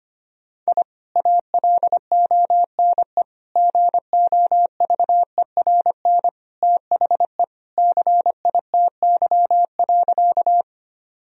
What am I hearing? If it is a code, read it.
Morse code, heard as a keyed tone, about 25 wpm, 700 Hz. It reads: I ALONE GOVERN THE CITY.
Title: Sentences from Top 1000 Words - Review Part 2 - 25wpm